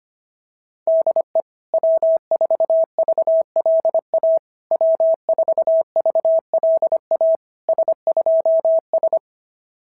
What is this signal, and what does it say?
Morse code, heard as a keyed tone, about 25 wpm, 660 Hz. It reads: DE W4VLA W4VLA S2S
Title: SOTA QSO Practice 25wpm